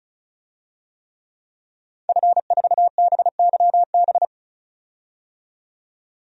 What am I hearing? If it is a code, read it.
Morse code, heard as a keyed tone, about 34 wpm, 700 Hz. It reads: F4BYB
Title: DXCC-100 Call Signs - 5 Characters - 35wpm